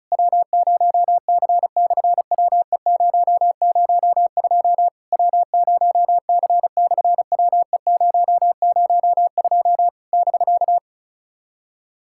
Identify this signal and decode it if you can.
Morse code, heard as a keyed tone, about 36 wpm, 700 Hz. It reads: W0C/WE002 W0C/WE002 <BK>
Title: SOTA QSO Practice 35wpm